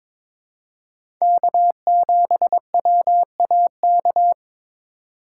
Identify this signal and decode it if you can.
Morse code, heard as a keyed tone, about 22 wpm, 700 Hz. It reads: K7WAK
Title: USA Call Signs - 3x Character Spacing - 22wpm